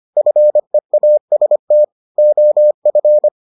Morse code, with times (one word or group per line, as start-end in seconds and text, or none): 0.2-1.8 FEAST
2.2-3.3 OF